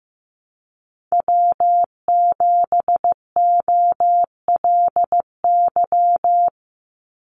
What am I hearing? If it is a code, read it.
Morse code, heard as a keyed tone, about 15 wpm, 700 Hz. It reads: W7OLY
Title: Call Signs Repeated 3x - 15wpm